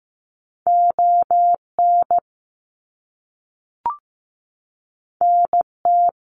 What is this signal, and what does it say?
Morse code, heard as a keyed tone, about 15 wpm, 700 Hz. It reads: ON NT
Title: Canadian Provinces and Territories 15wpm